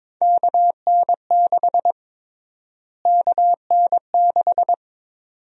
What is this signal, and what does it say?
Morse code, heard as a keyed tone, about 21 wpm, 700 Hz. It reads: KN6 KN6
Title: USA Call Sign Prefixes - Repeated 2x - 22wpm